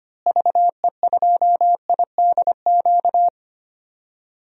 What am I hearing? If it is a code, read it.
Morse code, heard as a keyed tone, about 25 wpm, 700 Hz. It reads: VE2IDQ
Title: DXCC-100 Call Signs - 6 Characters - Repeated 2x - 25wpm